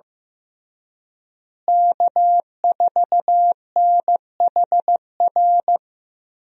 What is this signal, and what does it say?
Morse code, heard as a keyed tone, about 15 wpm, 700 Hz. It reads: K4NHR
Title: USA Call Signs - 5 Characters - Repeated 3x - 15wpm